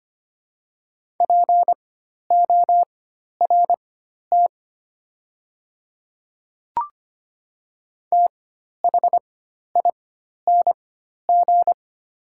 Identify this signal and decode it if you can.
Morse code, heard as a keyed tone, about 25 wpm, spaced as for 12 wpm, 700 Hz. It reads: PORT THING
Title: 4x Character Spacing - Top 300 Words - 25wpm